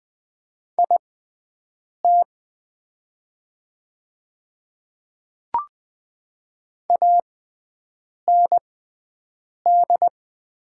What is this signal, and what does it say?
Morse code, heard as a keyed tone, about 20 wpm, spaced as for 7 wpm, 700 Hz. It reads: IT AND
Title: 6x Character Spacing - Top 300 Words - 20wpm